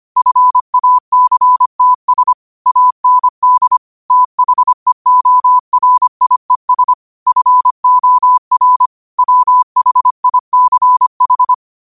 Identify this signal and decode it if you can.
Morse code, heard as a keyed tone, about 25 wpm, 1000 Hz. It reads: FACTS AND THEORIES FOR WHICH
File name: qotdfiles / 25wpm